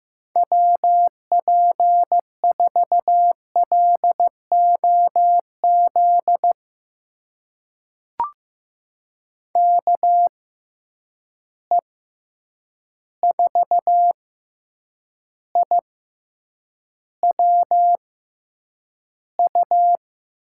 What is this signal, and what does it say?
Morse code, heard as a keyed tone, about 15 wpm, spaced as for 8 wpm, 700 Hz. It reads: WP4LOZ KE4IWU
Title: USA Call Signs - 6 Characters - 6x Character Spacing - 15wpm